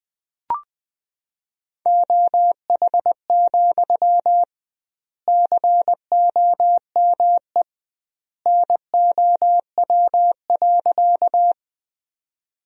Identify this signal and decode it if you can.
Morse code, heard as a keyed tone, about 20 wpm, 700 Hz. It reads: OH, COME NOW.
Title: 2x Word-Spacing - Sentences from Top 500 Words - 20wpm